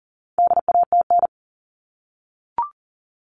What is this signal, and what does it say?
Morse code, heard as a keyed tone, about 40 wpm, 700 Hz. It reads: DATN
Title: All Cut Numbers - 4-Digit 40wpm